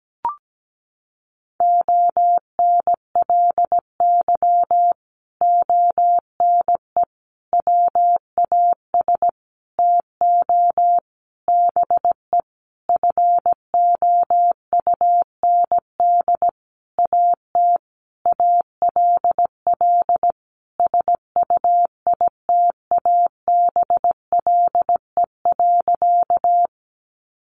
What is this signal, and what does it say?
Morse code, heard as a keyed tone, about 17 wpm, 700 Hz. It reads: ONLY ONE WAS TO BE FOUND AT ALL SUITABLE.
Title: Sentences from Top 1400 Words 17wpm